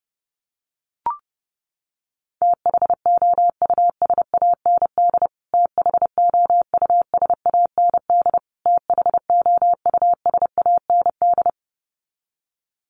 Morse code, heard as a keyed tone, about 30 wpm, 700 Hz. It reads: THOUSAND THOUSAND THOUSAND
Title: Top 500 Words Repeated 30wpm